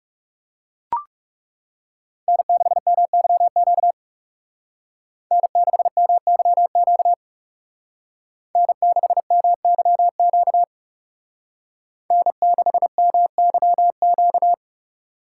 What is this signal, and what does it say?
Morse code, heard as a keyed tone, about 36 wpm, 700 Hz. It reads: N6MYQ N6MYQ N6MYQ N6MYQ
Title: Speed Racing - USA Call Signs 30wpm